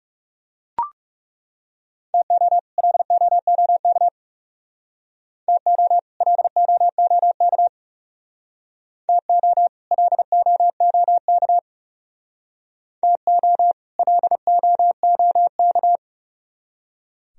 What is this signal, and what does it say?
Morse code, heard as a keyed tone, about 38 wpm, 700 Hz. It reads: TO LOOK TO LOOK TO LOOK TO LOOK